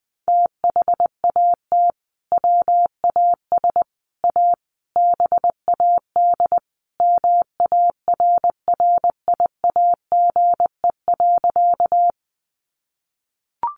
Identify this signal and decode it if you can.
Morse code, heard as a keyed tone, about 20 wpm, 700 Hz. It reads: THAT WAS A BAD MARRIAGE.
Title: Sentences from Top 800 Words 20wpm